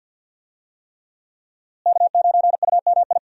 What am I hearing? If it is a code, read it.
Morse code, heard as a keyed tone, about 46 wpm, 700 Hz. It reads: K9UMI
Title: USA Call Signs - 5 Characters - Repeated 3x - 50wpm